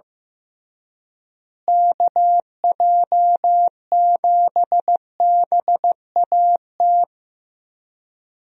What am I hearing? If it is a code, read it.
Morse code, heard as a keyed tone, about 15 wpm, 700 Hz. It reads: KJ7BAT
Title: USA Call Signs - 6 Characters - Repeated 3x - 15wpm